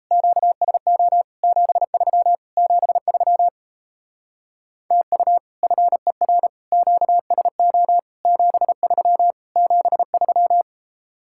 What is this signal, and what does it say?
Morse code, heard as a keyed tone, about 35 wpm, 700 Hz. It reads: QSO 73 73 TU FER QSO 73 73
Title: Speed-Racing Common QSO Phrases - 28wpm